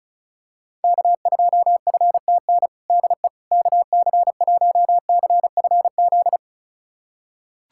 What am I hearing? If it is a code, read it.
Morse code, heard as a keyed tone, about 36 wpm, 700 Hz. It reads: K2FTN DE KC1CFZ
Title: USA Callsign DE Callsign 35wpm